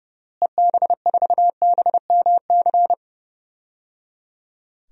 Morse code, heard as a keyed tone, about 29 wpm, 700 Hz. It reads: EB4BMC
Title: DXCC-100 Call Signs - 6 Characters - Repeated 3x - 30wpm